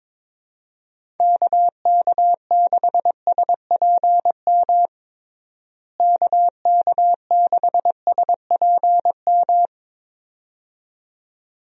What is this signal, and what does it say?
Morse code, heard as a keyed tone, about 23 wpm, 700 Hz. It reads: KK6SPM KK6SPM
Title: USA Call Signs - 6 Characters - Repeated 2x - 22wpm